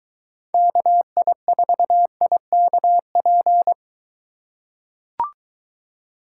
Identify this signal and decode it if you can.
Morse code, heard as a keyed tone, about 23 wpm, 700 Hz. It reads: KI4IKP